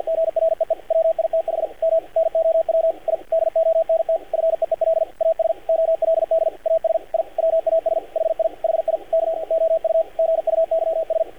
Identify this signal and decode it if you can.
Morse code, heard as a keyed tone, about 48 wpm, 590 Hz, with radio noise. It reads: GREE WITH M NOW I DONT FEEL AS OLD AS I WAS HI HI K OU GUYS
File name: archived / CRSnet